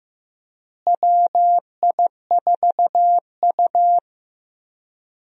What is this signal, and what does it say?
Morse code, heard as a keyed tone, about 15 wpm, 700 Hz. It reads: WI4U
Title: USA Call Signs - 4 Characters - Repeated 3x - 15wpm